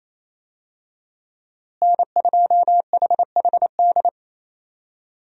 Morse code, heard as a keyed tone, about 28 wpm, 700 Hz. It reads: N2HHD
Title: USA Call Signs - 5 Characters - Repeated 2x - 28wpm